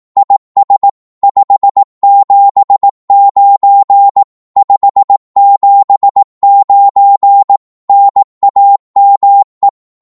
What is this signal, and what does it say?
Morse code, heard as a keyed tone, about 18 wpm, 800 Hz. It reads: IS 579 579 NAME
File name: Morse / qso-18wpm